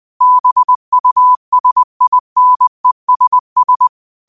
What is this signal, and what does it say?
Morse code, heard as a keyed tone, about 20 wpm, 1000 Hz. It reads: BUSINESS
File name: qotdfiles / 20wpm